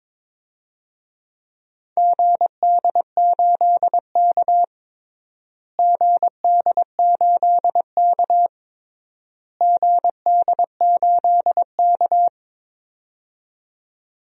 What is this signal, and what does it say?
Morse code, heard as a keyed tone, about 21 wpm, 700 Hz. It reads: GD8K GD8K GD8K
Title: DXCC-100 Call Signs - 4 Characters - Repeated 3x - 22wpm